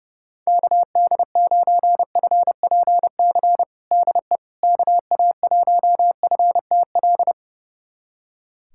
Morse code, heard as a keyed tone, about 32 wpm, 700 Hz. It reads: KD9FPC DE KA1FTL
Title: USA Callsign DE Callsign - 6 Characters - 30wpm